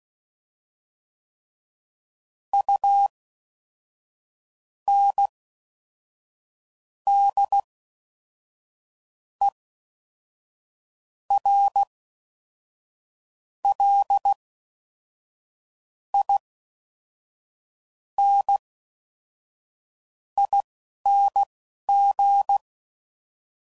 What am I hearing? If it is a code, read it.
Morse code, heard as a keyed tone, about 16 wpm, spaced as for 5 wpm, 800 Hz. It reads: UNDERLINING